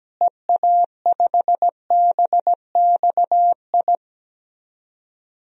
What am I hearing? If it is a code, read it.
Morse code, heard as a keyed tone, about 17 wpm, 700 Hz. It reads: EA5BXI
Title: DXCC-100 Call Signs - 6 Characters - Repeated 3x - 17wpm